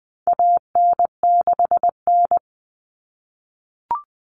Morse code, heard as a keyed tone, about 20 wpm, 700 Hz. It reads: AN6N
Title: Common Cut Numbers - 4-Digit 20wpm